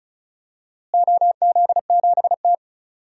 Morse code, heard as a keyed tone, about 36 wpm, 700 Hz. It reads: OZ7T
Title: DXCC-100 Call Signs - 4 Characters - Repeated 3x - 35wpm